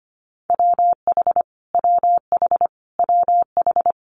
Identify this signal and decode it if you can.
Morse code, heard as a keyed tone, about 25 wpm, 700 Hz. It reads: WH WH WH
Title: Multi-letter Phonemes Repeated 3x 25wpm